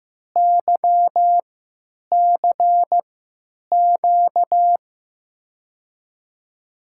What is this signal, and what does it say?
Morse code, heard as a keyed tone, about 15 wpm, spaced as for 9 wpm, 700 Hz. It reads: YCQ
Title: USA Call Signs - 6 Characters - 3x Character Spacing - 15wpm